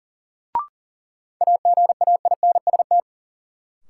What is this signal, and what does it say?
Morse code, heard as a keyed tone, about 40 wpm, 700 Hz. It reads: AGAINST
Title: Sets of 2 Words - New Word Drill - 40wpm